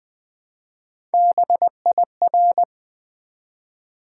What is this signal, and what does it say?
Morse code, heard as a keyed tone, about 20 wpm, 700 Hz. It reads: BIR
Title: Build A Word - 1 Letter At A Time - 3 to 5-Letter Words - 20wpm